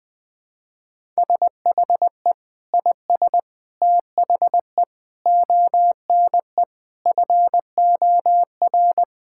Morse code, heard as a keyed tone, about 20 wpm, 700 Hz. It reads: SHE IS THE ONE FOR
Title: Sentences with 2-Word Phrases - 20wpm